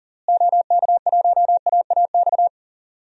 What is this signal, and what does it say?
Morse code, heard as a keyed tone, about 39 wpm, 700 Hz. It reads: OK1AAX
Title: DXCC-100 Call Signs - 6 Characters - Repeated 2x - 40wpm